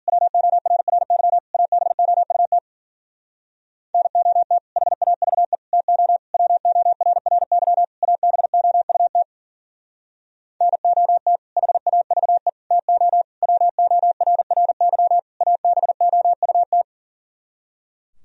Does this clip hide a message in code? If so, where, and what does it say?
Yes: Morse code from 0.1 to 16.8 s, heard as a keyed tone, about 52 wpm, 700 Hz. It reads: WORRY ABOUT NOT HAVE TO WORRY ABOUT NOT HAVE TO WORRY ABOUT